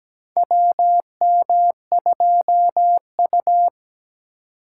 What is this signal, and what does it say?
Morse code, heard as a keyed tone, about 17 wpm, 700 Hz. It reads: WM2U
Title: USA Call Signs - 8x Character Spacing - 17wpm